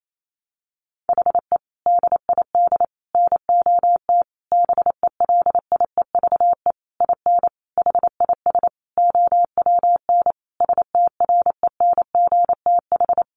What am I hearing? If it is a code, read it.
Morse code, heard as a keyed tone, about 28 wpm, 700 Hz. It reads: HE DID NOT BELIEVE IN HIS OWN STRENGTH
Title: Sentences from Top 1300 Words 28wpm